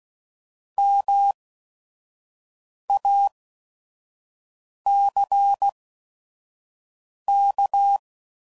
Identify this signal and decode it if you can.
Morse code, heard as a keyed tone, about 16 wpm, spaced as for 5 wpm, 800 Hz. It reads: MACK